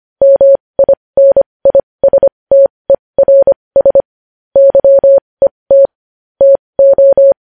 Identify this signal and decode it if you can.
Morse code, heard as a keyed tone, about 25 wpm, spaced as for 20 wpm, 550 Hz. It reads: MINISTERS YET TO
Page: news-headlines-in-morse-code-at-20-wpm